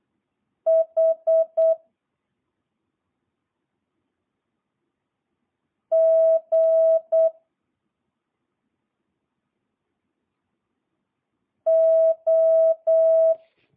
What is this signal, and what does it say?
Morse code, heard as a keyed tone, about 8 wpm, 650 Hz. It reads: H G O